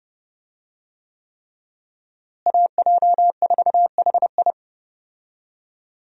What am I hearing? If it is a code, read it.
Morse code, heard as a keyed tone, about 31 wpm, 700 Hz. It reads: AJ4HI
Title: USA Call Signs - 5 Characters - Repeated 3x - 30wpm